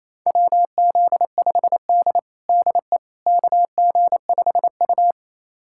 Morse code, heard as a keyed tone, about 28 wpm, 700 Hz. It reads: WZ5D DE KG5U
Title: USA Callsign DE Callsign - 4 Characters - 28wpm